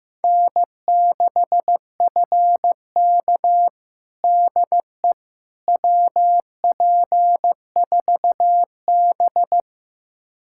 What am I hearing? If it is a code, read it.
Morse code, heard as a keyed tone, about 15 wpm, 700 Hz. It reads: N6FK DE WP4B
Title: USA Callsign DE Callsign - 4 Characters - 15wpm